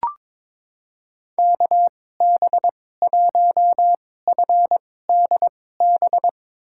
Morse code, heard as a keyed tone, about 22 wpm, spaced as for 16 wpm, 700 Hz. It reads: KB1FDB
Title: USA Call Signs - 2x Character Spacing - 22wpm